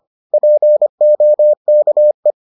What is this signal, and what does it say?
Morse code, heard as a keyed tone, about 25 wpm, 600 Hz. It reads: POKE